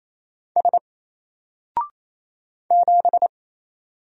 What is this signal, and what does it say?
Morse code, heard as a keyed tone, about 28 wpm, 700 Hz. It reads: S 7
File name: Single-Letter-Number-28wpm